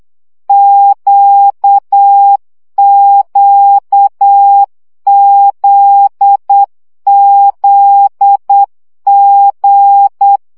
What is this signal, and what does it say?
Morse code, heard as a keyed tone, about 8 wpm, 800 Hz. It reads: QQZZG
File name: CW gyakorlas g z q 7 baud